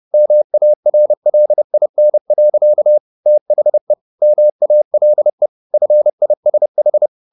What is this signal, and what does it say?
Morse code, heard as a keyed tone, about 30 wpm, 600 Hz. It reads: MARLIN. THE MALE FISH
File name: the old man and the sea / the old man and the sea 1 30wpm